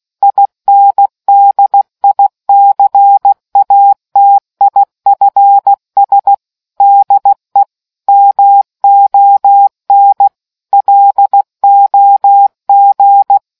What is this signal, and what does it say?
Morse code, heard as a keyed tone, about 16 wpm, 800 Hz. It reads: INDICATIFS DE MON LOG